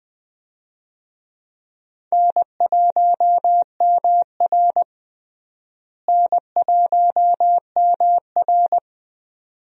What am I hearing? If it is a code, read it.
Morse code, heard as a keyed tone, about 20 wpm, 700 Hz. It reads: N1MR N1MR
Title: USA Call Signs - 4 Characters - Repeated 2x - 20wpm